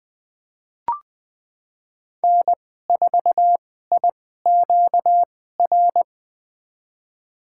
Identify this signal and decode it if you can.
Morse code, heard as a keyed tone, about 21 wpm, spaced as for 15 wpm, 700 Hz. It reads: N4IQR
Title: USA Call Signs - 5 Characters - 2x Character Spacing - 20wpm